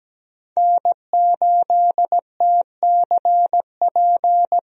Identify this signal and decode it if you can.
Morse code, heard as a keyed tone, about 17 wpm, 700 Hz. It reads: N8TCP
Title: USA Call Signs - 5 Characters - 17wpm